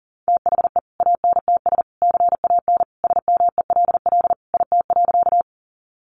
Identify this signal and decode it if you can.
Morse code, heard as a keyed tone, about 40 wpm, 700 Hz. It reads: THE ANTS CAN SMELL IT.
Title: Additional Easy Phrases 40wpm